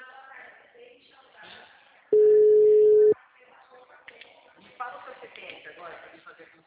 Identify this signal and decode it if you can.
Ringback tone (Europe, 425 Hz), with speech.